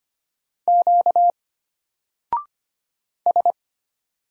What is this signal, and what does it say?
Morse code, heard as a keyed tone, about 25 wpm, 700 Hz. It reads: Q S
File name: Single-Letter-Number-25wpm